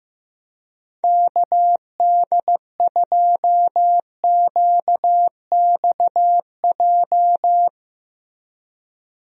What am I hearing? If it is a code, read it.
Morse code, heard as a keyed tone, about 15 wpm, 700 Hz. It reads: KD2QXJ
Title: USA Call Signs - 6 Characters - Repeated 2x - 15wpm